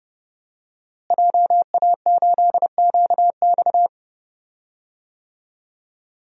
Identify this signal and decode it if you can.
Morse code, heard as a keyed tone, about 31 wpm, 700 Hz. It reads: JA8QX
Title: DXCC-100 Call Signs - 5 Characters - Repeated 2x - 30wpm